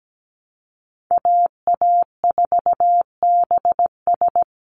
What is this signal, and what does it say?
Morse code, heard as a keyed tone, about 17 wpm, 700 Hz. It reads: AA4BS
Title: USA Call Signs 17wpm